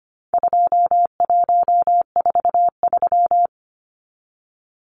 Morse code, heard as a keyed tone, about 25 wpm, 700 Hz. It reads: 2143
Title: Numbers - 4-Digit 25wpm